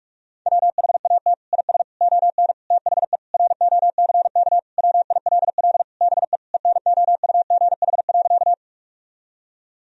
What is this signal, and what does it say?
Morse code, heard as a keyed tone, about 41 wpm, 700 Hz. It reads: WHAT IS ON THE ROCK WILL BE ENOUGH.
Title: Sentences from Top 1000 Words - Review Part 2 - 45wpm